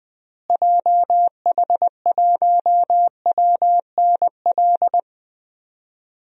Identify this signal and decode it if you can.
Morse code, heard as a keyed tone, about 20 wpm, 700 Hz. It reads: JH1WNL
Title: DXCC-100 Call Signs - 6 Characters - Repeated 3x - 20wpm